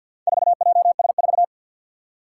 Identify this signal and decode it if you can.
Morse code, heard as a keyed tone, about 50 wpm, 700 Hz. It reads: 4JS4
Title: Four Challenging Letter-Number Combinations 50wpm